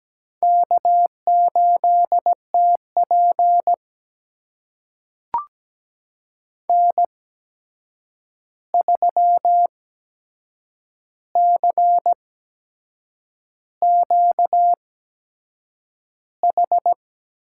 Morse code, heard as a keyed tone, about 17 wpm, spaced as for 7 wpm, 700 Hz. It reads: K8TP N3CQH
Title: USA Call Signs - 8x Character Spacing - 17wpm